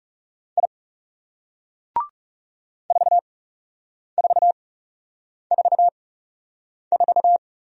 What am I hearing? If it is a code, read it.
Morse code, heard as a keyed tone, about 37 wpm, spaced as for 8 wpm, 700 Hz. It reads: I 4444
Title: Speed-Racing Single Letter-Number 30wpm